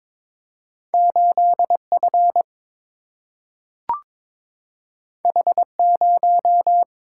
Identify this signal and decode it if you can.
Morse code, heard as a keyed tone, about 22 wpm, 700 Hz. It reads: 8 F H 0
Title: Single-Letter Single-Number 22wpm